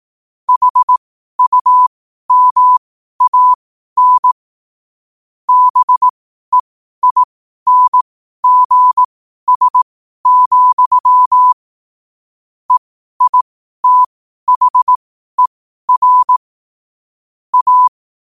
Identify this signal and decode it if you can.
Morse code, heard as a keyed tone, about 18 wpm, spaced as for 12 wpm, 1000 Hz. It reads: HUMAN BEINGS, EITHER A
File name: qotdfiles / 12wpm